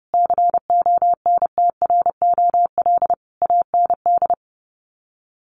Morse code, heard as a keyed tone, about 30 wpm, 700 Hz. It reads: CONTROL AND
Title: Sets of 5 Words Encore2 30wpm